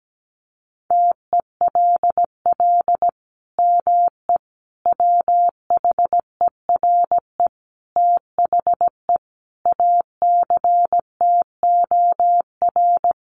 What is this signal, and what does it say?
Morse code, heard as a keyed tone, about 17 wpm, 700 Hz. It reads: TELL ME WHERE THE ACTOR
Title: Sentences from Top 1500 Words 17wpm